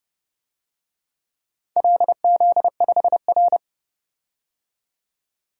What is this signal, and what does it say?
Morse code, heard as a keyed tone, about 28 wpm, 700 Hz. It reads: LZ5R
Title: DXCC-100 Call Signs - 4 Characters - 30wpm